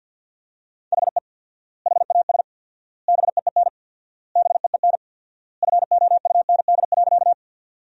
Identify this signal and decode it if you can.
Morse code, heard as a keyed tone, about 50 wpm, 700 Hz. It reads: HE HAS BEEN BEEN FOUND.
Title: 4x Word-Spacing - Sentences from Top 300 Words - 50wpm